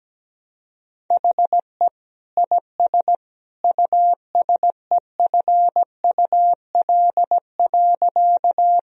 Morse code, heard as a keyed tone, about 17 wpm, 700 Hz. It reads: HE IS USEFUL.
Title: Sentences from Top 1000 Words - Review Part 4 - 17wpm